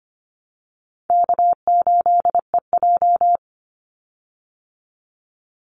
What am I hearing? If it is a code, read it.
Morse code, heard as a keyed tone, about 25 wpm, 700 Hz. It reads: K8EJ
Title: Easy Call Signs Repeated 3x - 25wpm